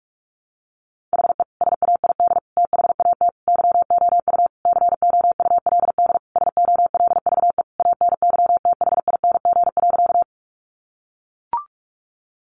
Morse code, heard as a keyed tone, about 44 wpm, 700 Hz. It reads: HE SAID THAT YOU COULD SOLVE ANYTHING.
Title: Sentences from Top 1000 Words 45wpm